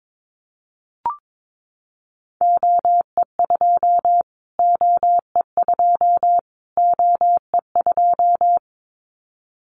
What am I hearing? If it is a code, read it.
Morse code, heard as a keyed tone, about 22 wpm, 700 Hz. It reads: OE2 OE2 OE2
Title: Callsign Prefixes for Top 25 DXCC - Repeated 3x - 22wpm